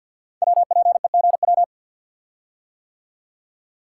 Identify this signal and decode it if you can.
Morse code, heard as a keyed tone, about 50 wpm, 700 Hz. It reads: WPEGW
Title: Code Groups - 5 Letter:Numbers - 50wpm